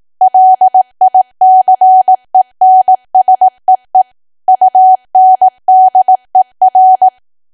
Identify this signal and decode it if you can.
Morse code, heard as a keyed tone, about 18 wpm, 750 Hz. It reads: LICENSEE UNDER
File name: cwa / W1AW-Level2 / 18WPM